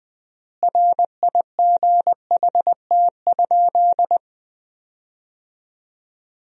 Morse code, heard as a keyed tone, about 20 wpm, 700 Hz. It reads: RIGHT?
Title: 3x Word-Spacing - Sentences from Top 200 Words - 20wpm